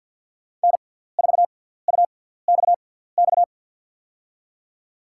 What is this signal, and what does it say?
Morse code, heard as a keyed tone, about 53 wpm, spaced as for 17 wpm, 700 Hz. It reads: N4UXX
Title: USA Call Signs - 5 Characters - 6x Character Spacing - 50wpm